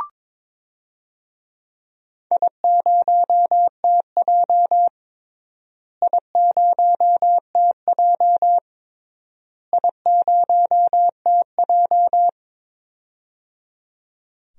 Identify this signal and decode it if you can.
Morse code, heard as a keyed tone, about 22 wpm, 700 Hz. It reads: I0TJ I0TJ I0TJ
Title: DXCC-100 Call Signs - 4 Characters - Repeated 3x - 22wpm